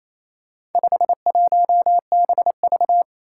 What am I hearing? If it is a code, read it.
Morse code, heard as a keyed tone, about 28 wpm, 700 Hz. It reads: 51BV
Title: Four Challenging Letter-Number Combinations 28wpm